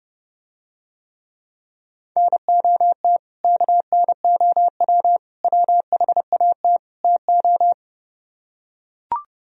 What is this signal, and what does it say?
Morse code, heard as a keyed tone, about 30 wpm, 700 Hz. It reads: NOT KNOW WHAT TO